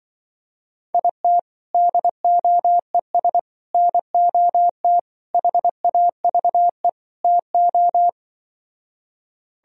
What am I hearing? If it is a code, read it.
Morse code, heard as a keyed tone, about 24 wpm, 700 Hz. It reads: IT DOES NOT HAVE TO
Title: Speed-Racing - Sets of 5 Words 20wpm